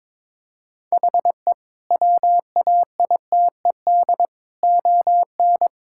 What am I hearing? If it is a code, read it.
Morse code, heard as a keyed tone, about 22 wpm, 700 Hz. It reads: HE WAITED ON
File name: practice / 22wpm / Sentences with 4-Word Phrases - 22wpm